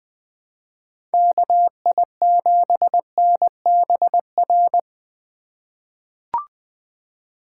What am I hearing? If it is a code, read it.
Morse code, heard as a keyed tone, about 20 wpm, 700 Hz. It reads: KI7NBR
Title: USA Call Signs - 6 Characters - 4x Character Spacing - 20wpm